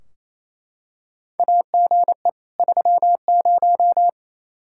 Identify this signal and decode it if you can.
Morse code, heard as a keyed tone, about 28 wpm, 700 Hz. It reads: AGE 30
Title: Practice QSO - Exchange 3 - More Info - 28wpm